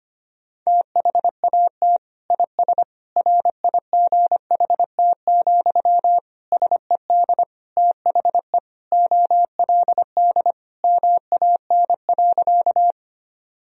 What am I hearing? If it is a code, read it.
Morse code, heard as a keyed tone, about 25 wpm, 700 Hz. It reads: THAT IS RIGHT, SED THE OLD MAN.
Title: Sentences from Top 200 Words and Abbreviations 25wpm